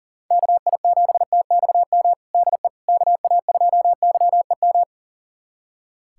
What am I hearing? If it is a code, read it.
Morse code, heard as a keyed tone, about 40 wpm, 700 Hz. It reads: KI7TXM DE KA2YEM
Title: USA Callsign DE Callsign - 6 Characters - 40wpm